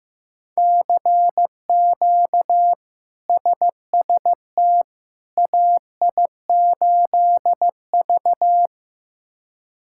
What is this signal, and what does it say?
Morse code, heard as a keyed tone, about 15 wpm, 700 Hz. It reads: CQ SST AI8V
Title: K1USN SST QSO Practice - 15wpm